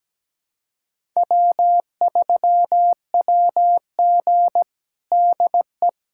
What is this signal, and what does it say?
Morse code, heard as a keyed tone, about 17 wpm, 700 Hz. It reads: W3WG DE
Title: USA Callsign DE Callsign - 4 Characters - 17wpm